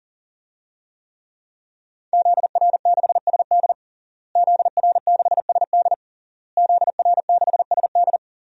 Morse code, heard as a keyed tone, about 40 wpm, 700 Hz. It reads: ZR6SD ZR6SD ZR6SD
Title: DXCC-100 Call Signs - 5 Characters - Repeated 3x - 40wpm